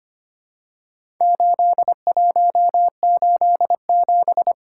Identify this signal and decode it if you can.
Morse code, heard as a keyed tone, about 25 wpm, 700 Hz. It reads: 8187
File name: practice / 25wpm / Four Letter-Number Combinations - 25wpm